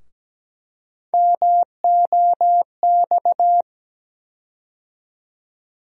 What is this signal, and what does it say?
Morse code, heard as a keyed tone, about 17 wpm, 700 Hz. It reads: MOX
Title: USA Call Sign Postfixes - 3 Letters 17wpm